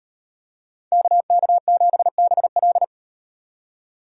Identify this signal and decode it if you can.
Morse code, heard as a keyed tone, about 38 wpm, 700 Hz. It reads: KK7BL